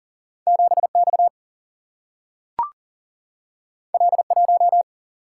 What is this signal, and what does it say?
Morse code, heard as a keyed tone, about 39 wpm, 700 Hz. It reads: 7X L1
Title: Single-Letter Single-Number 40wpm